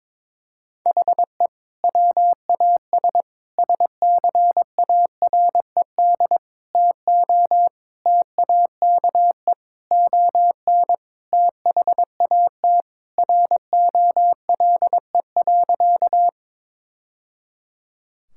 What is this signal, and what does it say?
Morse code, heard as a keyed tone, about 21 wpm, 700 Hz. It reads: HE WAS SCARED TO TAKE ON THAT ROLE.
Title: Sentences from Top 1000 Words - Review Part 6 - 22wpm